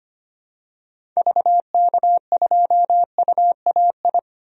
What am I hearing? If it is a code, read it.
Morse code, heard as a keyed tone, about 25 wpm, 700 Hz. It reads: VK2UAI
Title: DXCC-100 Call Signs - 6 Characters - 25wpm